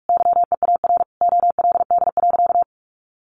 Morse code, heard as a keyed tone, about 44 wpm, 700 Hz. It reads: YEAR OLD.
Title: Sentences from Top 1400 Words 45wpm